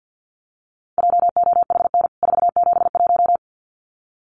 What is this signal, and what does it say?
Morse code, heard as a keyed tone, about 48 wpm, 700 Hz. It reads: JOHN 471
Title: CWT Exchange 50wpm